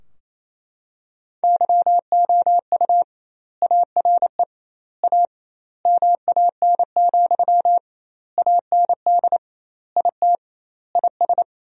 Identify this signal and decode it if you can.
Morse code, heard as a keyed tone, about 28 wpm, 700 Hz. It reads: YOU ARE A MAN, AND IT IS
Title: 2x Word-Spacing - Sentences from Top 200 Words - 28wpm